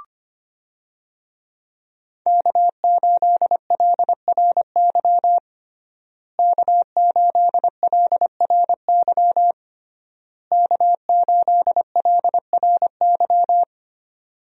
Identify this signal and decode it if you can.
Morse code, heard as a keyed tone, about 25 wpm, 700 Hz. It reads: K8LRY K8LRY K8LRY
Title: USA Call Signs - 5 Characters - Repeated 3x - 25wpm